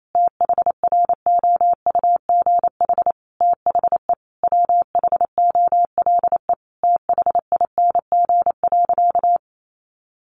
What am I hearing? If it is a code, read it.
Morse code, heard as a keyed tone, about 28 wpm, 700 Hz. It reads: THROUGH THE WHOLE THING.
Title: Sentences from Top 400 Words 28wpm